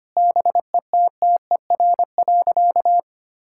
Morse code, heard as a keyed tone, about 25 wpm, 700 Hz. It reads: BETTER.
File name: practice / 25wpm / Difficult Sentences with Hints - 25wpm